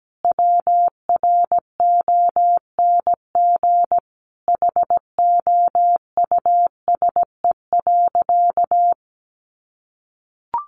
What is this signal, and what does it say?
Morse code, heard as a keyed tone, about 17 wpm, 700 Hz. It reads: WRONG HOUSE.
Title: Sentences from Top 900 Words 17wpm